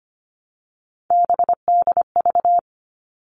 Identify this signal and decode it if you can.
Morse code, heard as a keyed tone, about 25 wpm, 700 Hz. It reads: BDV
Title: All Cut Numbers - 3-Digit 25wpm